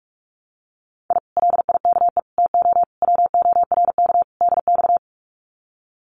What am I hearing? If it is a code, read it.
Morse code, heard as a keyed tone, about 46 wpm, 700 Hz. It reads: I LIKE TO WORK DX
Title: Common QSO Phrases Encore 45wpm